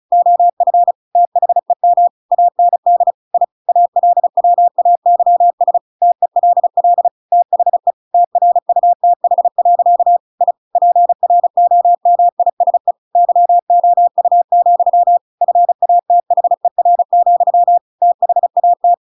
Morse code, heard as a keyed tone, about 35 wpm, 700 Hz. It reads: OF THEM AND I ALWAYS TELL THE TRUTH. I PROMISE YOU, FATHER, THAT
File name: Pinocchio / Pinocchio ch8 35 35 ik1hge